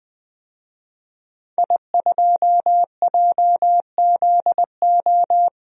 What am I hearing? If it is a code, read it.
Morse code, heard as a keyed tone, about 20 wpm, 700 Hz. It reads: I2JZO
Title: DXCC-100 Call Signs - 5 Characters - 20wpm